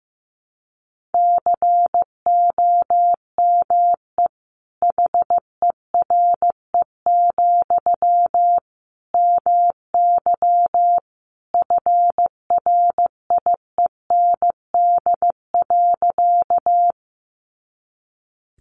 Morse code, heard as a keyed tone, about 15 wpm, 700 Hz. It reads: COME HERE, MY FRIEND.
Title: Sentences from Top 300 Words 15wpm